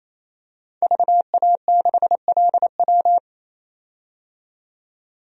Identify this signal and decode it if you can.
Morse code, heard as a keyed tone, about 28 wpm, 700 Hz. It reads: VA6LW
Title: DXCC-100 Call Signs - 5 Characters - 28wpm